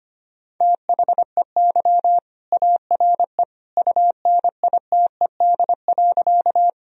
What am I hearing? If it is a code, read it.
Morse code, heard as a keyed tone, about 25 wpm, 700 Hz. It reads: THEY ARE UNITED.
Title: Sentences from Top 1000 Words - Review Part 4 - 25wpm